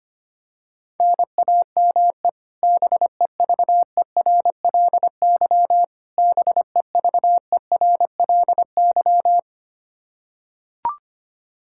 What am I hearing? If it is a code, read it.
Morse code, heard as a keyed tone, about 25 wpm, 700 Hz. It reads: NAME BEVERLY BEVERLY
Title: Practice QSO - Exchange 1 - Beginning - 25wpm